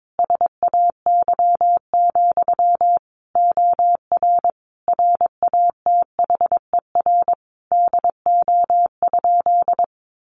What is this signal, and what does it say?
Morse code, heard as a keyed tone, about 22 wpm, 700 Hz. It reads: SAY, OR RATHER DO?
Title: Sentences from Top 700 Words 22wpm